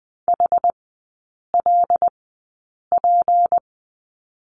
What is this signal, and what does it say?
Morse code, heard as a keyed tone, about 20 wpm, spaced as for 8 wpm, 690 Hz. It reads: HLP
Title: 166 Lesson 27e - Number 7 - Three Characters 20wpm